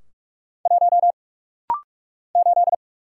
Morse code, heard as a keyed tone, about 48 wpm, 700 Hz. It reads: 1 8
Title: Single Letter-Number - Rapid-Fire 45wpm